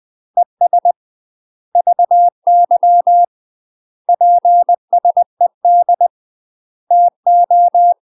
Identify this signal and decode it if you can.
Morse code, heard as a keyed tone, about 20 wpm, 700 Hz. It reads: ES VY PSED TO
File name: UK MASTER QSOs TRAINING / 20WPM / UK MASTER QSOs 0007